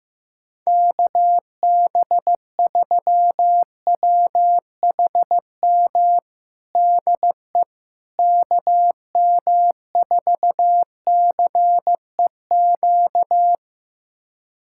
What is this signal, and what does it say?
Morse code, heard as a keyed tone, about 15 wpm, 700 Hz. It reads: KB3WHM DE KM4CEQ
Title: USA Callsign DE Callsign - 6 Characters - 15wpm